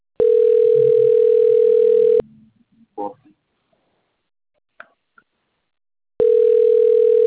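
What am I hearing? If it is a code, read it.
Ringback tone (North America), 440 + 480 Hz.
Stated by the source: Outgoing telephone call recorded on the Piracicaba phone system.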